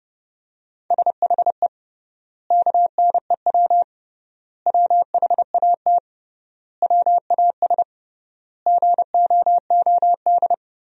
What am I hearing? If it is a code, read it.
Morse code, heard as a keyed tone, about 32 wpm, 700 Hz. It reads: SHE KNEW WHAT WAS GOOD
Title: 3x Word-Spacing - Sentences from Top 400 Words - 30wpm